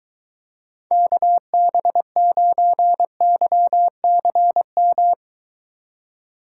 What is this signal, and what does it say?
Morse code, heard as a keyed tone, about 23 wpm, 700 Hz. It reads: KB9YCM